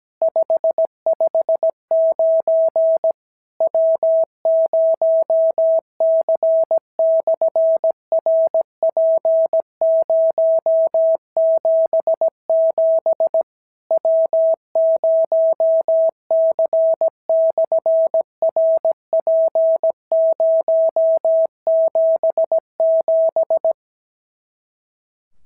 Morse code, heard as a keyed tone, about 17 wpm, 650 Hz. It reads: 559 W0C/RP077 W0C/RP077
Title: SOTA QSO Practice 17wpm